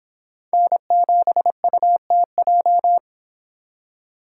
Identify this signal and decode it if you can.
Morse code, heard as a keyed tone, about 26 wpm, 700 Hz. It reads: N7UTJ